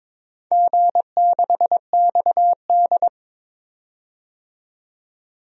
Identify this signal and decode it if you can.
Morse code, heard as a keyed tone, about 22 wpm, 700 Hz. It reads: G6XD
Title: DXCC-100 Call Signs - 4 Characters - 22wpm